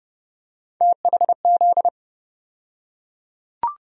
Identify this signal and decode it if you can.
Morse code, heard as a keyed tone, about 30 wpm, 700 Hz. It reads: THZ
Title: USA Call Sign Postfixes - 3 Letters 30wpm